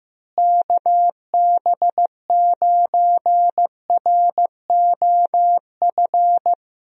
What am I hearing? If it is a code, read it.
Morse code, heard as a keyed tone, about 15 wpm, 700 Hz. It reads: KB9ROF
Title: USA Call Signs - 6 Characters - Repeated 3x - 15wpm